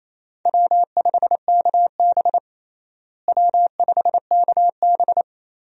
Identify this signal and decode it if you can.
Morse code, heard as a keyed tone, about 28 wpm, 700 Hz. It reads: W5KB W5KB
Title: USA Call Signs - 4 Characters - Repeated 2x - 28wpm